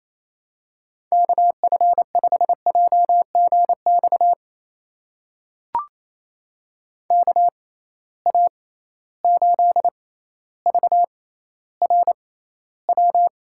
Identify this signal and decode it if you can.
Morse code, heard as a keyed tone, about 27 wpm, spaced as for 14 wpm, 700 Hz. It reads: KF5JGX KA8VRW
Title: USA Call Signs - 6 Characters - 6x Character Spacing - 28wpm